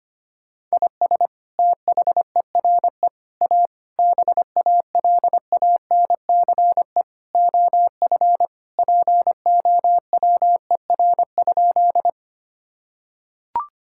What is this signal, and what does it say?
Morse code, heard as a keyed tone, about 25 wpm, 700 Hz. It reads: IS THERE A BALANCE OF POWER?
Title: Sentences from Top 1000 Words - Review Part 6 - 25wpm